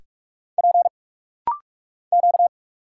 Morse code, heard as a keyed tone, about 46 wpm, 690 Hz. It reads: P Q
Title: Single Letters - Rapid-Fire 45wpm